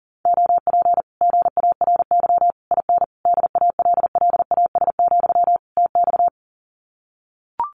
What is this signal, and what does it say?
Morse code, heard as a keyed tone, about 40 wpm, 700 Hz. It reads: OP GARY IN DALLAS, TX
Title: Common QSO Phrases Encore 40wpm